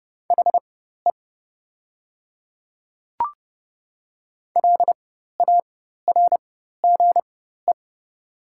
Morse code, heard as a keyed tone, about 31 wpm, spaced as for 14 wpm, 700 Hz. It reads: HE LARGE
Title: 4x Character Spacing - Top 300 Words - 30wpm